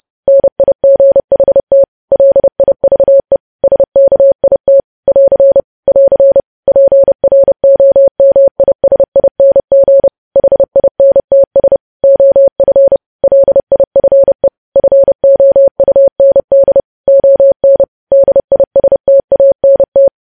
Morse code, heard as a keyed tone, about 30 wpm, 550 Hz. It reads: NIGHT LIVE SKIT <AR> <AR> PROMISING HINTS OF LIFE FOUND ON DISTANT
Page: news headlines in morse code at 30 wpm